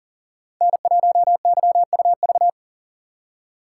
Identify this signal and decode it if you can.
Morse code, heard as a keyed tone, about 37 wpm, 700 Hz. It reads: N1YUV
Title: USA Call Signs - 4x Character Spacing - 40wpm